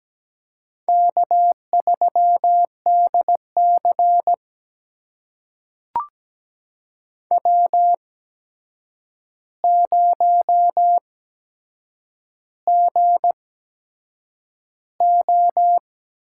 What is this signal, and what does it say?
Morse code, heard as a keyed tone, about 17 wpm, spaced as for 7 wpm, 700 Hz. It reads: K3DC W0GO
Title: USA Call Signs - 4 Characters - 8x Character Spacing - 17wpm